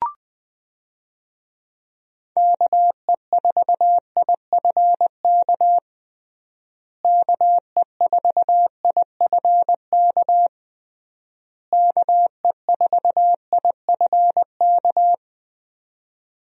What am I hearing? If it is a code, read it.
Morse code, heard as a keyed tone, about 21 wpm, 700 Hz. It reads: KE4IFK KE4IFK KE4IFK
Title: USA Call Signs - 6 Characters - Repeated 3x - 20wpm